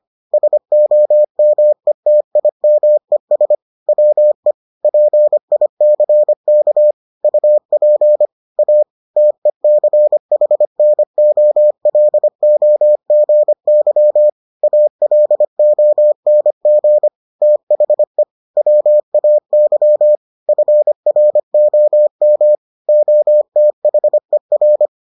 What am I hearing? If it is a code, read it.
Morse code, heard as a keyed tone, about 25 wpm, 600 Hz. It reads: SOMETIMES WE PICK UP A TECHNOLOGY ALONG THE WAY FROM OTHER